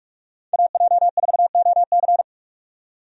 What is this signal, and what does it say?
Morse code, heard as a keyed tone, about 46 wpm, 700 Hz. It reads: AJ4OC
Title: USA Call Signs - 5 Characters - 45wpm